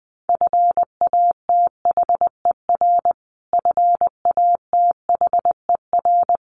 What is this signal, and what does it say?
Morse code, heard as a keyed tone, about 19 wpm, 700 Hz. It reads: FATHER FATHER
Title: Top 200 Words Repeated 20wpm